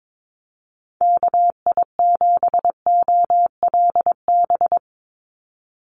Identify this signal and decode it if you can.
Morse code, heard as a keyed tone, about 22 wpm, 700 Hz. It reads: KI7OLB
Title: USA Call Signs 22wpm